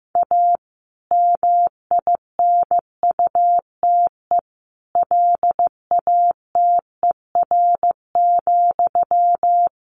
Morse code, heard as a keyed tone, about 15 wpm, 700 Hz. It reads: A MINUTE LATER,
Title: Sentences from Top 1500 Words 15wpm